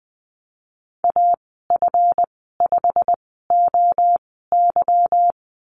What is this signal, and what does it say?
Morse code, heard as a keyed tone, about 20 wpm, spaced as for 14 wpm, 700 Hz. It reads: AF5OY
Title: 207 Lesson 33g - Letter j - Call Signs 20wpm